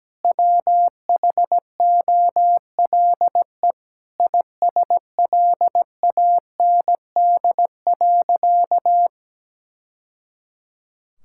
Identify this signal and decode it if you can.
Morse code, heard as a keyed tone, about 17 wpm, 700 Hz. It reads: WHOLE ISLAND.
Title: Sentences from Top 1000 Words - Review Part 5 - 17wpm